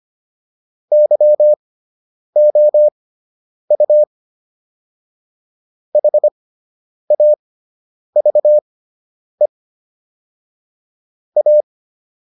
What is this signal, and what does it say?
Morse code, heard as a keyed tone, about 25 wpm, spaced as for 9 wpm, 600 Hz. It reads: YOU HAVE A